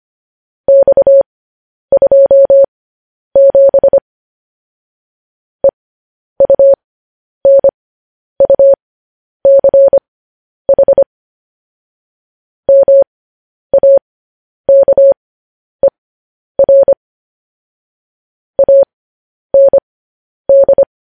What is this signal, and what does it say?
Morse code, heard as a keyed tone, about 25 wpm, spaced as for 10 wpm, 550 Hz. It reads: X27 EUNUCH MAKER AND
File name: podcast / News headlines-1x-25x10